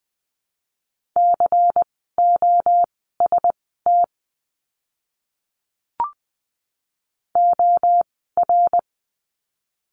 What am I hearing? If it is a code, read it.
Morse code, heard as a keyed tone, about 20 wpm, spaced as for 14 wpm, 700 Hz. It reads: COST OR
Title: 101 Lesson 17f - Letter u - Words 20wpm